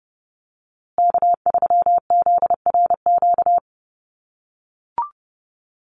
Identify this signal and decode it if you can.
Morse code, heard as a keyed tone, about 30 wpm, 700 Hz. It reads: K3ZRQ
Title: USA Call Signs Repeated 3x 30wpm